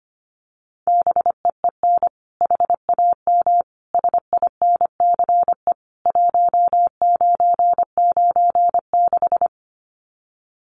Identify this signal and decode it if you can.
Morse code, heard as a keyed tone, about 25 wpm, 700 Hz. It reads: BEEN HAM SINCE 1996
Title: Common QSO Phrases 25wpm